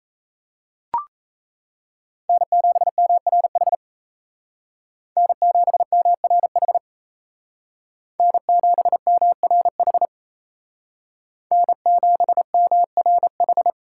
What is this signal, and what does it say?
Morse code, heard as a keyed tone, about 33 wpm, 700 Hz. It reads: N7MRH N7MRH N7MRH N7MRH
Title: Speed Racing - USA 1x3 Call Signs 28wpm